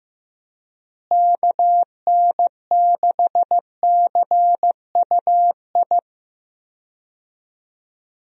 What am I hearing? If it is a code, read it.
Morse code, heard as a keyed tone, about 15 wpm, 700 Hz. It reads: KN6CUI
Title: USA Call Signs - 6 Characters - Repeated 3x - 15wpm